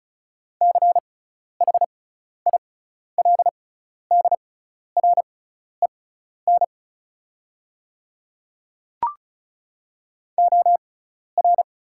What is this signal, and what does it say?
Morse code, heard as a keyed tone, about 36 wpm, spaced as for 12 wpm, 700 Hz. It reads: CHILDREN OR
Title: 6x Character Spacing - Top 500 Words - 35wpm